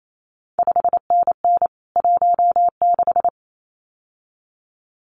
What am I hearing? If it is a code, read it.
Morse code, heard as a keyed tone, about 28 wpm, 700 Hz. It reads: 5NN 16
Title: CQ WW DX Contest Exchange - 28wpm